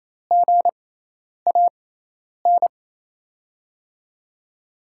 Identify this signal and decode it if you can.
Morse code, heard as a keyed tone, about 27 wpm, spaced as for 10 wpm, 700 Hz. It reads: GAN
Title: USA Call Signs - 5 Characters - 6x Character Spacing - 28wpm